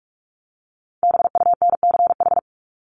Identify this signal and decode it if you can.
Morse code, heard as a keyed tone, about 44 wpm, 700 Hz. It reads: BUNCH
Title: Top 1751-2000 Words 45wpm